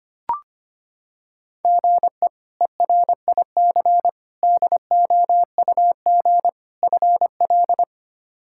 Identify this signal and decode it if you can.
Morse code, heard as a keyed tone, about 25 wpm, 700 Hz. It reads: GE ERIC DOUG FL
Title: K1USN SST QSO Practice - 25wpm